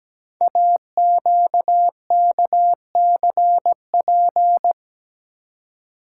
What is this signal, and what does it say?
Morse code, heard as a keyed tone, about 17 wpm, 700 Hz. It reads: AQKCP
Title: Code Groups - 5 Letters - 17wpm